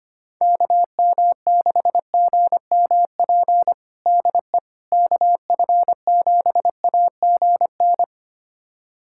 Morse code, heard as a keyed tone, about 25 wpm, 700 Hz. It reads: KM6GMP DE KF7AGN
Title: USA Callsign DE Callsign - 6 Characters - 25wpm